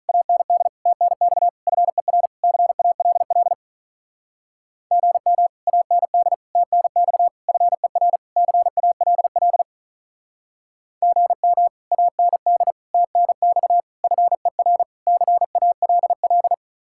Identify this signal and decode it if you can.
Morse code, heard as a keyed tone, about 44 wpm, 700 Hz. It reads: AND TNX FER CALL GM AND TNX FER CALL GM AND TNX FER CALL
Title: Speed-Racing Common QSO Phrases - 35wpm